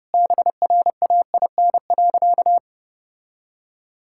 Morse code, heard as a keyed tone, about 30 wpm, 700 Hz. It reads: BRAIN.
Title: Sentences from Top 1600 Words 30wpm